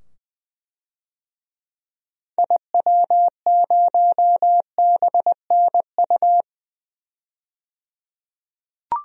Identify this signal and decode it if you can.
Morse code, heard as a keyed tone, about 20 wpm, 700 Hz. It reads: IW0BNU
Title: DXCC-100 Call Signs - 6 Characters - 20wpm